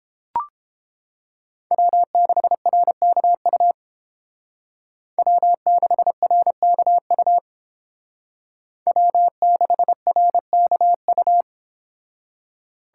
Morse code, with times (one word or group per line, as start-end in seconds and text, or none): 1.7-3.7 W6RKU
5.2-7.4 W6RKU
8.9-11.4 W6RKU